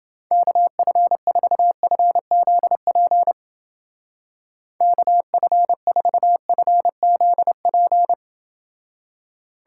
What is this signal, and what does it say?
Morse code, heard as a keyed tone, about 29 wpm, 700 Hz. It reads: KF4FZP KF4FZP